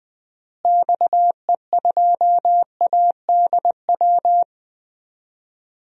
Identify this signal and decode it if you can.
Morse code, heard as a keyed tone, about 21 wpm, 700 Hz. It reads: XE2ADW
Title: DXCC-100 Call Signs - 6 Characters - Repeated 3x - 20wpm